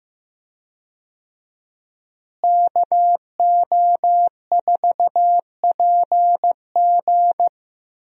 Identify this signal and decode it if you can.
Morse code, heard as a keyed tone, about 15 wpm, 700 Hz. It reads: KO4PG
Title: USA Call Signs - 5 Characters - Repeated 3x - 15wpm